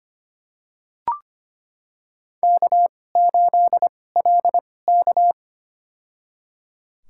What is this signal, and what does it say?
Morse code, heard as a keyed tone, about 25 wpm, spaced as for 18 wpm, 700 Hz. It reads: K8LK
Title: USA Call Signs - 4 Characters - 2x Character Spacing - 25wpm